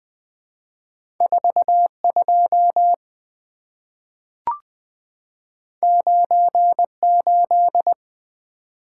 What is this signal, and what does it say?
Morse code, heard as a keyed tone, about 20 wpm, 690 Hz. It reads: 42 98
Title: Two Letter-Number Combinations - 20wpm